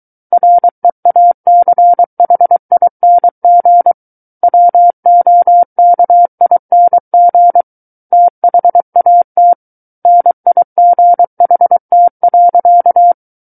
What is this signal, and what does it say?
Morse code, heard as a keyed tone, about 23 wpm, 700 Hz. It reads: REACHING WOKING THAT NIGHT.
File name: Book One / 14-Chapter 14 - 23 WPM